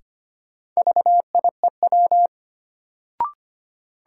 Morse code, heard as a keyed tone, about 25 wpm, 700 Hz. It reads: VIEW
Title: Sentences from Top 900 Words - New Word Drill - 25wpm